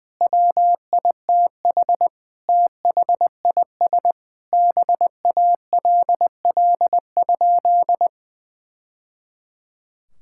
Morse code, heard as a keyed tone, about 20 wpm, 700 Hz. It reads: WITH THIS BALL?
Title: Sentences from Top 1000 Words - Review Part 6 - 20wpm